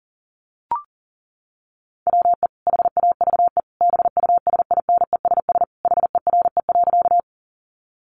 Morse code, heard as a keyed tone, about 40 wpm, 700 Hz. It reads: WE HAVE BUSINESS HERE.
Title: Sentences from Top 600 Words 40wpm